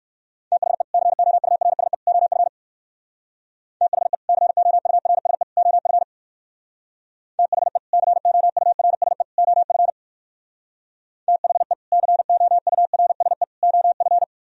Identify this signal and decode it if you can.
Morse code, heard as a keyed tone, about 64 wpm, 700 Hz. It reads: THE COURSE OF THE COURSE OF THE COURSE OF THE COURSE OF